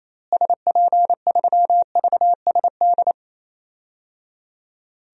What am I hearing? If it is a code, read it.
Morse code, heard as a keyed tone, about 28 wpm, 700 Hz. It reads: SP3VSD
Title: DXCC-100 Call Signs - 6 Characters - Repeated 3x - 28wpm